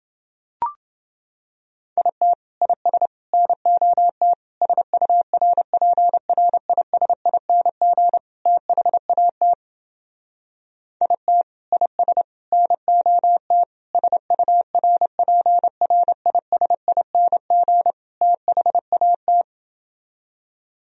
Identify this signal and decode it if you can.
Morse code, heard as a keyed tone, about 28 wpm, 700 Hz. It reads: IT IS NOT SURPRISING THAT IT IS NOT SURPRISING THAT